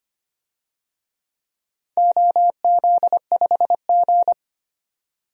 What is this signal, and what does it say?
Morse code, heard as a keyed tone, about 25 wpm, 700 Hz. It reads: OZ5G
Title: DXCC-100 Call Signs - 4 Characters - 25wpm